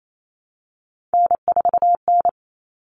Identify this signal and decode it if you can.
Morse code, heard as a keyed tone, about 28 wpm, 700 Hz. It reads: N4N
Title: All Cut Numbers - 3-Digit 28wpm